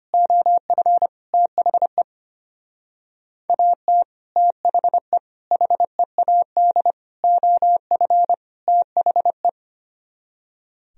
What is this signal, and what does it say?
Morse code, heard as a keyed tone, about 31 wpm, 700 Hz. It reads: OF THE AT THE HEAD OF THE